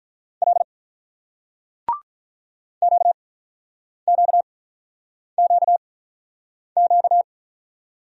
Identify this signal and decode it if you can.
Morse code, heard as a keyed tone, about 49 wpm, spaced as for 9 wpm, 700 Hz. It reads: L QQQQ